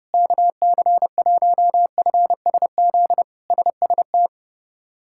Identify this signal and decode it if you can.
Morse code, heard as a keyed tone, about 30 wpm, 700 Hz. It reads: KC1FSZ SST
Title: K1USN SST QSO Practice - 30wpm